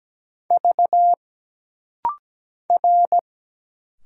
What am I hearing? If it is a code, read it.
Morse code, heard as a keyed tone, about 17 wpm, 700 Hz. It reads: V R
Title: Single Letters - Rapid-Fire 17wpm